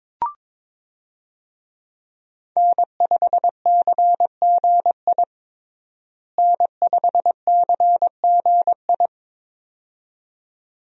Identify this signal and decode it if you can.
Morse code, heard as a keyed tone, about 22 wpm, 700 Hz. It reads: N5CGI N5CGI
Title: USA Call Signs - 5 Characters - Repeated 2x - 22wpm